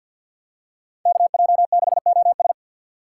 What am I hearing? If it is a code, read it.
Morse code, heard as a keyed tone, about 48 wpm, 700 Hz. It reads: KJ6OS
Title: USA Call Signs - 5 Characters - 4x Character Spacing - 50wpm